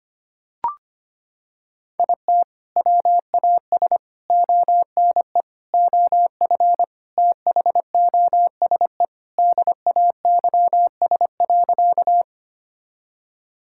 Morse code, heard as a keyed tone, about 25 wpm, 700 Hz. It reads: IT WAS ONE OF THOSE DAYS.
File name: practice / 25wpm / Sentences with 3-Word Phrases - 25wpm